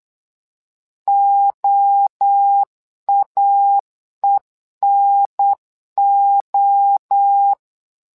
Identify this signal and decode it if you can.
Morse code, heard as a keyed tone, about 8 wpm, 800 Hz. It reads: OAENO
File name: LEZIONI / 3 MWRO / 3 MWRO 8wpm 2606